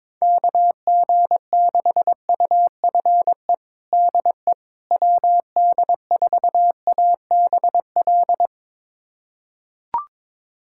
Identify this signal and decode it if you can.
Morse code, heard as a keyed tone, about 22 wpm, 700 Hz. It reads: KG6UFE DE WD4ABL
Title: USA Callsign DE Callsign 22wpm